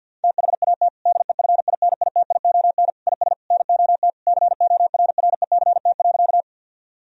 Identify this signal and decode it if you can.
Morse code, heard as a keyed tone, about 46 wpm, 700 Hz. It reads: THAT DEFINITION IS NOT CORRECT.
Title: Sentences from Top 1000 Words - Review Part 4 - 50wpm